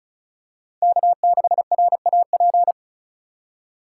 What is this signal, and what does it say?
Morse code, heard as a keyed tone, about 34 wpm, 700 Hz. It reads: K6RAP
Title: USA Call Signs - 5 Characters - 2x Character Spacing - 35wpm